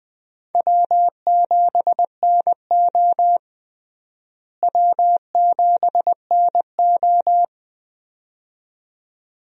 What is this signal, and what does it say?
Morse code, heard as a keyed tone, about 20 wpm, 700 Hz. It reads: W7NO W7NO
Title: USA Call Signs - 4 Characters - Repeated 2x - 20wpm